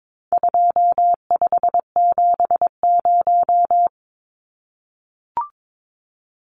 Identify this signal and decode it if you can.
Morse code, heard as a keyed tone, about 22 wpm, 700 Hz. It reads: 2570
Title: Numbers - 4-Digit 22wpm